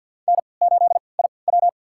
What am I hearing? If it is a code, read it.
Morse code, heard as a keyed tone, about 51 wpm, spaced as for 28 wpm, 700 Hz. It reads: N8IW
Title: USA Call Signs - 4 Characters - 3x Character Spacing - 50wpm